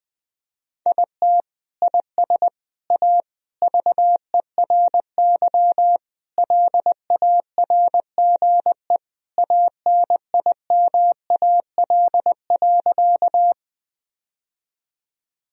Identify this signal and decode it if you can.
Morse code, heard as a keyed tone, about 20 wpm, 700 Hz. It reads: IT IS A VERY LARGE ANIMAL.
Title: Sentences from Top 1000 Words - Review Part 1 - 20wpm